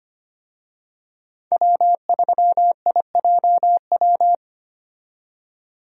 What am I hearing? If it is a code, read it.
Morse code, heard as a keyed tone, about 25 wpm, 690 Hz. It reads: W3IJW
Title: USA Call Signs - 5 Characters - 25wpm